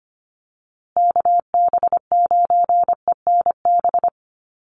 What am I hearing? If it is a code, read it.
Morse code, heard as a keyed tone, about 25 wpm, 700 Hz. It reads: KB9ENB
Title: USA Call Signs Repeated 3x 25wpm